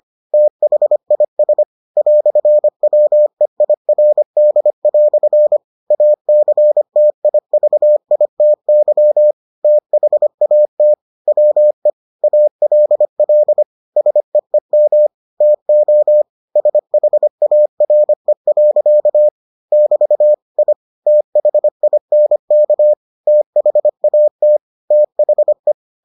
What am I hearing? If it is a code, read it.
Morse code, heard as a keyed tone, about 25 wpm, 600 Hz. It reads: THIS "WEIRD" ACTIVITY THAT WE ALL SEEM TO SHARE. <BT> I THINK THAT THE